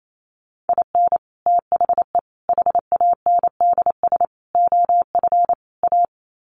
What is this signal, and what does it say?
Morse code, heard as a keyed tone, about 28 wpm, 700 Hz. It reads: IN THE HANDS OF A
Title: Sets of 5 Words Encore 28wpm